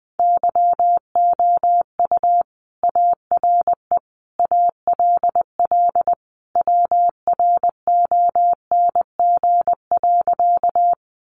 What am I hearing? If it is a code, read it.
Morse code, heard as a keyed tone, about 20 wpm, 700 Hz. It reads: YOU ARE ALL WRONG.
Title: Sentences from Top 900 Words 20wpm